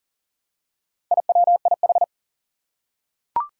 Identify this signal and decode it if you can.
Morse code, heard as a keyed tone, about 40 wpm, 700 Hz. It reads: IWIH
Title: Four Challenging Letter-Number Combinations 40wpm